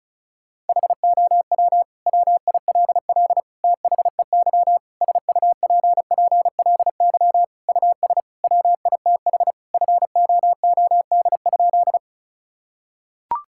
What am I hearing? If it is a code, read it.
Morse code, heard as a keyed tone, about 36 wpm, 690 Hz. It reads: HOW WILL THEY SUPPLY US WITH FOOD?
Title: Sentences from Top 1000 Words - Review Part 5 - 35wpm